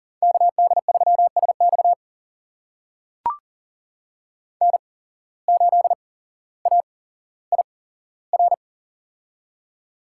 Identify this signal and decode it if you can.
Morse code, heard as a keyed tone, about 40 wpm, spaced as for 17 wpm, 700 Hz. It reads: KD3SX N8AIR
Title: USA Call Signs - 5 Characters - 8x Character Spacing - 40wpm